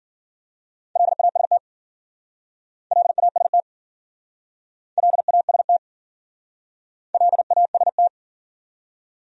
Morse code, heard as a keyed tone, about 50 wpm, 700 Hz. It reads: LAST LAST LAST LAST
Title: Speed-Racing Top 500 Words 40wpm